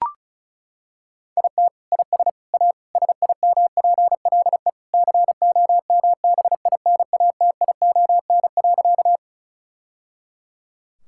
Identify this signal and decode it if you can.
Morse code, heard as a keyed tone, about 36 wpm, 700 Hz. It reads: IT IS A SIMPLE COMBINATION.
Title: Sentences from Top 1000 Words - Review Part 6 - 35wpm